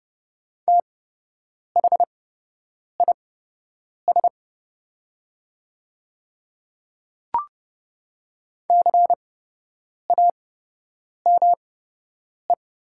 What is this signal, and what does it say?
Morse code, heard as a keyed tone, about 29 wpm, spaced as for 8 wpm, 690 Hz. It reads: THIS CAME
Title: 8x Character Spacing - Top 500 Words - 30wpm